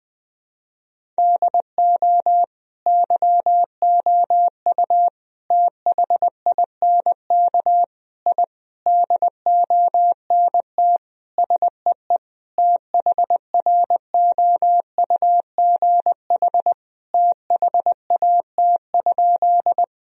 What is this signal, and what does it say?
Morse code, heard as a keyed tone, about 20 wpm, 700 Hz. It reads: DO YOU THINK I DONT SEE THROUGH THAT?
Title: Sentences from Top 1000 Words - Review Part 1 - 20wpm